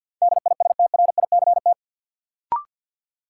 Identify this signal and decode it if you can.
Morse code, heard as a keyed tone, about 52 wpm, 700 Hz. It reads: DISTRICT
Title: Sentences from Top 1600 Words - New Word Drill Alt - 50wpm